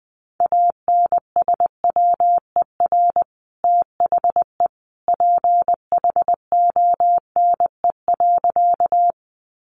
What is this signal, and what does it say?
Morse code, heard as a keyed tone, about 20 wpm, 700 Hz. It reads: ANSWER THE PHONE.
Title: Sentences from Top 1100 Words 20wpm